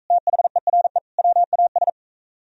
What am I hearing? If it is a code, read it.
Morse code, heard as a keyed tone, about 44 wpm, 700 Hz. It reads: THERE WAS